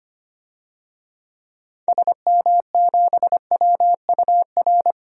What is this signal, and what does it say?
Morse code, heard as a keyed tone, about 25 wpm, 700 Hz. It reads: SM7WUR
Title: DXCC-100 Call Signs - 6 Characters - 25wpm